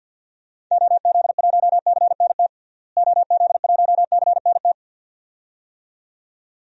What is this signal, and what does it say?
Morse code, heard as a keyed tone, about 47 wpm, 700 Hz. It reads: OZ1CNT OZ1CNT
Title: DXCC-100 Call Signs - 6 Characters - Repeated 2x - 50wpm